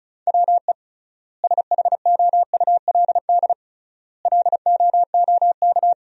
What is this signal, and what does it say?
Morse code, heard as a keyed tone, about 36 wpm, 700 Hz. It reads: WE SHOULD LOOK
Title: 3x Word-Spacing - Sentences from Top 300 Words - 35wpm